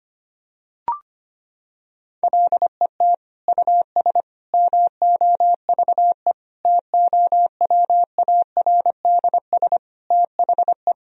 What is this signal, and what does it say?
Morse code, heard as a keyed tone, about 25 wpm, 700 Hz. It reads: LET US MOVE TOWARDS THE
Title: Sentences from Top 1000 Words - Review Part 2 - 25wpm